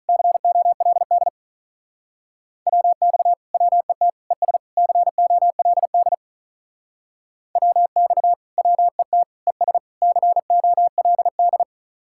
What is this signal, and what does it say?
Morse code, heard as a keyed tone, about 44 wpm, 700 Hz. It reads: COLD WX WET ES COLD WX WET ES COLD